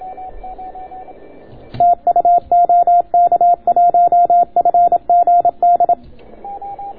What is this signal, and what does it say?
Morse code, heard as a keyed tone, about 27 wpm, 690 Hz. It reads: TU OK1FGD